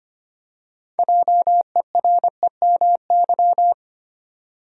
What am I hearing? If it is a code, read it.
Morse code, heard as a keyed tone, about 25 wpm, 700 Hz. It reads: JEREMY
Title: USA Common Names - 25wpm